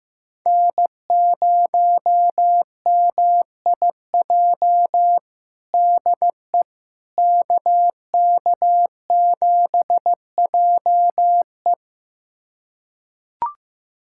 Morse code, heard as a keyed tone, about 15 wpm, 700 Hz. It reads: N0MIJ DE KK7JE
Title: USA Callsign DE Callsign - 5 Characters - 15wpm